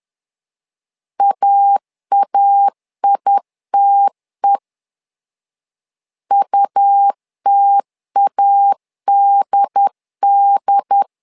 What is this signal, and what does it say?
Morse code, heard as a keyed tone, about 10 wpm, 800 Hz. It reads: AAITE UTADD
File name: LEZIONI / 2 UHDN / 2 UHD 10wpm 1826